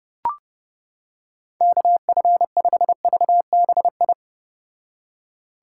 Morse code, heard as a keyed tone, about 32 wpm, 700 Hz. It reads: KF5VBI